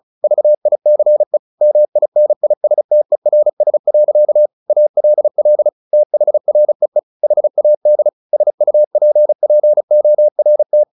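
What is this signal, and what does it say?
Morse code, heard as a keyed tone, about 34 wpm, 600 Hz. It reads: VICE MINISTERS. ALL THREE HAD SUPPORT
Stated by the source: Morse practice at 35 wpm from CW med Gnister, sending Ken Follett's novel Never in English.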